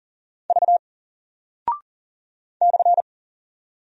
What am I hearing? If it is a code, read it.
Morse code, heard as a keyed tone, about 40 wpm, 700 Hz. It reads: V /
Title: Single Character (Letter:Number:Punctuation:Prosign) 40wpm